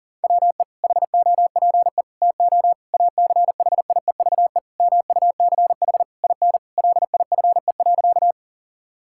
Morse code, heard as a keyed tone, about 37 wpm, 700 Hz. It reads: WE HOPE TO ACHIEVE MUCH IN LIFE.
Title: Sentences from Top 1000 Words - Review Part 6 - 40wpm